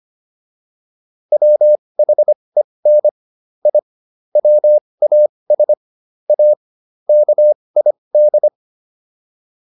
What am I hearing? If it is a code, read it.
Morse code, heard as a keyed tone, about 25 wpm, spaced as for 20 wpm, 600 Hz. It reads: WHEN I WAS A KID